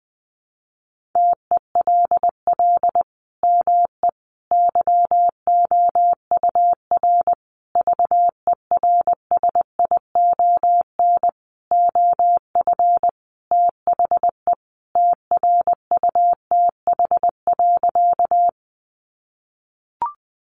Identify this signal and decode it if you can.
Morse code, heard as a keyed tone, about 20 wpm, 700 Hz. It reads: TELL ME YOUR VERSION OF THE TRUTH.
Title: Sentences from Top 1400 Words 20wpm